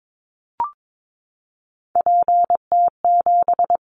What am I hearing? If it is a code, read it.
Morse code, heard as a keyed tone, about 22 wpm, 700 Hz. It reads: PT7
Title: Callsign Prefixes for Top 50 DXCC - 22wpm